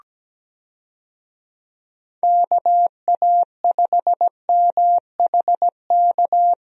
Morse code, heard as a keyed tone, about 17 wpm, 700 Hz. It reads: KA5MHK
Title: USA Call Signs - 6 Characters - 17wpm